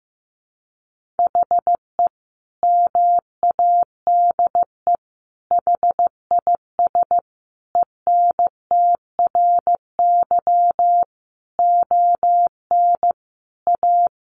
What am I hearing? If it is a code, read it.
Morse code, heard as a keyed tone, about 15 wpm, 700 Hz. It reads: HE MADE HIS ENTRY ON A
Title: Sentences from Top 1500 Words 15wpm